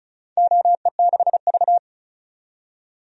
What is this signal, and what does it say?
Morse code, heard as a keyed tone, about 36 wpm, 700 Hz. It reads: OE6V
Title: DXCC-100 Call Signs - 4 Characters - 35wpm